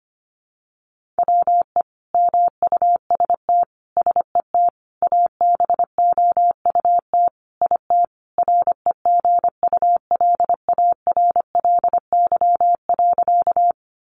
Morse code, heard as a keyed tone, about 25 wpm, 700 Hz. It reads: WE MUST SET ABOUT IT REGULARLY.
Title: Sentences from Top 1400 Words 25wpm